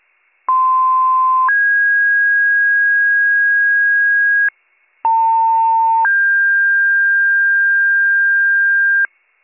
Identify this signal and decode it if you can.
Two-tone page, heard as a one-second tone, then a three-second tone, about 1040 then 1650 Hz, about 900 then 1600 Hz. It is heard from 0.5 to 9.1 s.